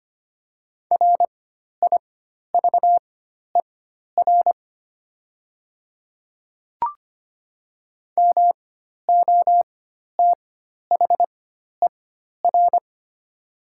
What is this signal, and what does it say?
Morse code, heard as a keyed tone, about 25 wpm, spaced as for 12 wpm, 700 Hz. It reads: RIVER MOTHER
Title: 4x Character Spacing - Top 300 Words - 25wpm